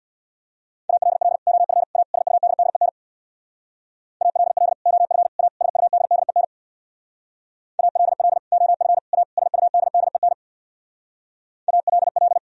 Morse code, heard as a keyed tone, about 64 wpm, 700 Hz. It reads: ALL OF A SUDDEN ALL OF A SUDDEN ALL OF A SUDDEN ALL